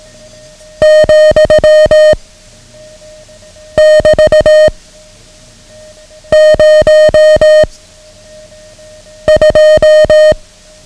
Morse code, heard as a keyed tone, about 18 wpm, 620 Hz. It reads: , <BT> 0 2
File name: ham radio / CodeQuick / Code Quick 09